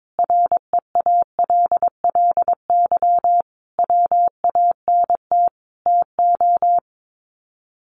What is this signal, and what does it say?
Morse code, heard as a keyed tone, about 22 wpm, 700 Hz. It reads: REALLY WANT TO
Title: Sets of 5 Words Encore2 22wpm